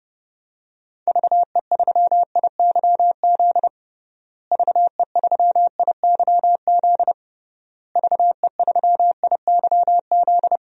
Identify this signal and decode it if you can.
Morse code, heard as a keyed tone, about 30 wpm, 700 Hz. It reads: VE3IYZ VE3IYZ VE3IYZ
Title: DXCC-100 Call Signs - 6 Characters - Repeated 3x - 30wpm